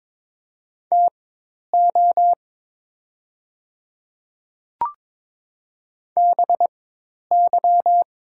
Morse code, heard as a keyed tone, about 22 wpm, spaced as for 10 wpm, 700 Hz. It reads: TO BY
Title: 4x Character Spacing - Top 100 Words - 22wpm